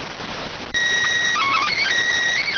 SSTV (slow-scan television), PD 290.